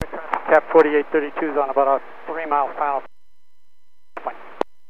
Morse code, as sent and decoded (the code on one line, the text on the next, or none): none